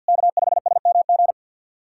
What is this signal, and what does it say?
Morse code, heard as a keyed tone, about 52 wpm, 690 Hz. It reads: K5SMG
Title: SOTA QSO Practice 50wpm